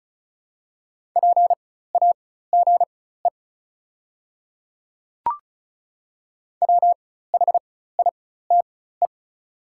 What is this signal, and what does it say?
Morse code, heard as a keyed tone, about 36 wpm, spaced as for 16 wpm, 700 Hz. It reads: PAGE WHITE
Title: 4x Character Spacing - Top 300 Words - 35wpm